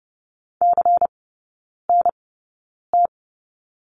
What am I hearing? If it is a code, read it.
Morse code, heard as a keyed tone, about 30 wpm, spaced as for 9 wpm, 690 Hz. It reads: CNT
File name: Lesson 16e - Letter c - Three Characters 30wpm